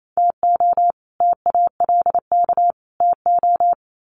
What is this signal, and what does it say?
Morse code, heard as a keyed tone, about 28 wpm, 700 Hz. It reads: TO TALK TO
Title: Sets of 3 Words 28wpm